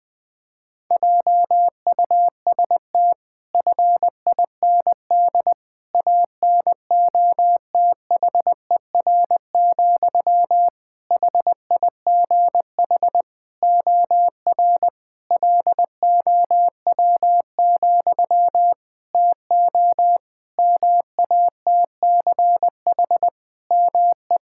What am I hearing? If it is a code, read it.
Morse code, heard as a keyed tone, about 20 wpm, 700 Hz. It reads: JUST FIND ANOTHER, HIGH OR LOW, TO MATCH ME
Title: Sentences from Top 1600 Words 20wpm